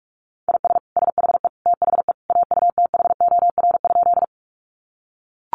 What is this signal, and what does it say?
Morse code, heard as a keyed tone, about 44 wpm, 700 Hz. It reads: IS SHE THE AUTHOR?
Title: Sentences from Top 1200 Words 45wpm